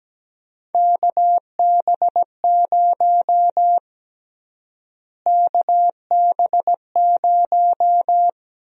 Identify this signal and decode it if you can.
Morse code, heard as a keyed tone, about 17 wpm, 700 Hz. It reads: KB0 KB0
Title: USA Call Sign Prefixes - Repeated 2x - 17wpm